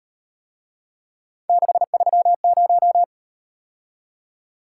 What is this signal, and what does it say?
Morse code, heard as a keyed tone, about 38 wpm, 700 Hz. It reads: 630